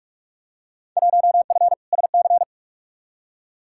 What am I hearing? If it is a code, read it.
Morse code, heard as a keyed tone, about 45 wpm, 700 Hz. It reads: 1F SC
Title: Field Day Exchanges Encore - 45wpm